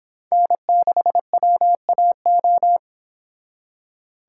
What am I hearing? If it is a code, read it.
Morse code, heard as a keyed tone, about 25 wpm, 700 Hz. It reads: N6WAO